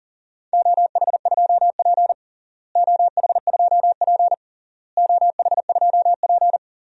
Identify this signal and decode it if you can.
Morse code, heard as a keyed tone, about 40 wpm, 700 Hz. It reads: OH2P OH2P OH2P
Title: DXCC-100 Call Signs - 4 Characters - Repeated 3x - 40wpm